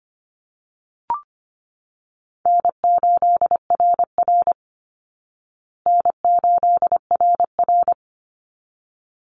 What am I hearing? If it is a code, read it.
Morse code, heard as a keyed tone, about 25 wpm, 700 Hz. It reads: N8RR N8RR
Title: Easy Call Signs Repeated 2x - 25wpm